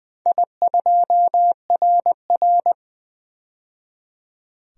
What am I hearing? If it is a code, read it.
Morse code, heard as a keyed tone, about 20 wpm, 700 Hz. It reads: I2RR
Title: DXCC-100 Call Signs - 4 Characters - 20wpm